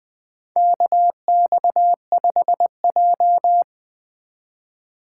Morse code, heard as a keyed tone, about 20 wpm, 700 Hz. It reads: KX5J
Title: USA Call Signs - 4x Character Spacing - 20wpm